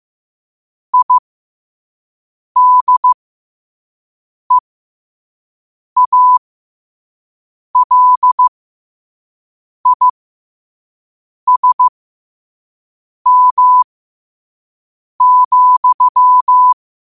Morse code, heard as a keyed tone, about 15 wpm, spaced as for 5 wpm, 1000 Hz. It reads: IDEALISM,
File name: qotdfiles / 5wpm